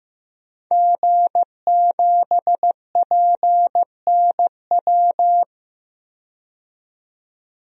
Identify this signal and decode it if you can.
Morse code, heard as a keyed tone, about 15 wpm, 700 Hz. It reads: G7PNW
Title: DXCC-100 Call Signs - 5 Characters - 15wpm